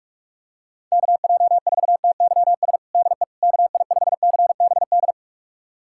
Morse code, heard as a keyed tone, about 46 wpm, 700 Hz. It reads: KJ4TYS DE KI5CBD
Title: USA Callsign DE Callsign - 6 Characters - 45wpm